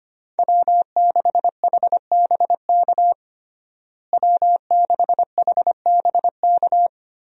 Morse code, heard as a keyed tone, about 25 wpm, 700 Hz. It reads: W6HBK W6HBK
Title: USA Call Signs - 5 Characters - Repeated 3x - 25wpm